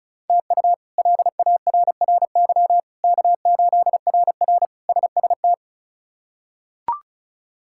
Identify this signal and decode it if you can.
Morse code, heard as a keyed tone, about 34 wpm, 700 Hz. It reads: TU LARRY K8RR SST
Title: K1USN SST QSO Practice - 35wpm